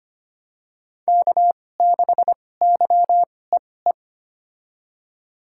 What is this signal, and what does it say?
Morse code, heard as a keyed tone, about 25 wpm, spaced as for 18 wpm, 700 Hz. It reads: K6YEE
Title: USA Call Signs - 5 Characters - 2x Character Spacing - 25wpm